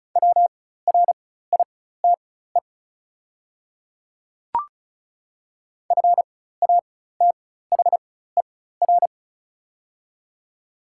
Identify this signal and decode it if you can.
Morse code, heard as a keyed tone, about 36 wpm, spaced as for 16 wpm, 700 Hz. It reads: WRITE FATHER
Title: 4x Character Spacing - Top 200 Words - 35wpm